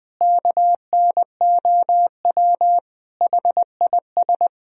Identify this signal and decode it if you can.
Morse code, heard as a keyed tone, about 21 wpm, 700 Hz. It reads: KNOW HIS
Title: Sentences from Top 1000 Words - Review Part 3 - 20wpm